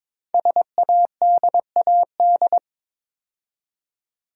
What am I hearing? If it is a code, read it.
Morse code, heard as a keyed tone, about 22 wpm, 700 Hz. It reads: SADAD
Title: Code Groups - 5 Letters - 22wpm